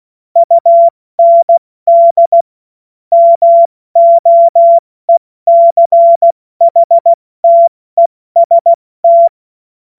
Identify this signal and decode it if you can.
Morse code, heard as a keyed tone, about 16 wpm, 680 Hz. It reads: UND MOECHTEST
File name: Morsekurs / Witze / Witz 25. April 70 F80